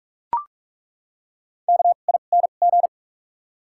Morse code, heard as a keyed tone, about 41 wpm, spaced as for 31 wpm, 700 Hz. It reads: KING
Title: 2x Character Spacing - Top 500 Words - 45wpm